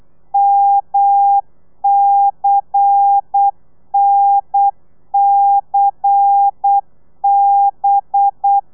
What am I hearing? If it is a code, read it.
Morse code, heard as a keyed tone, about 8 wpm, 800 Hz. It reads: MCNCB
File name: telegrafia / i2tfj / Lez11 35